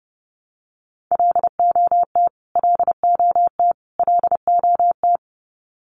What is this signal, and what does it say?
Morse code, heard as a keyed tone, about 30 wpm, 700 Hz. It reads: LOT LOT LOT
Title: Top 500 Words Repeated 30wpm